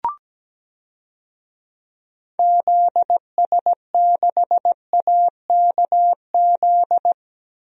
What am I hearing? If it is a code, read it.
Morse code, heard as a keyed tone, about 17 wpm, 700 Hz. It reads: ZS6AKZ
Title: DXCC-100 Call Signs - 6 Characters - Repeated 3x - 17wpm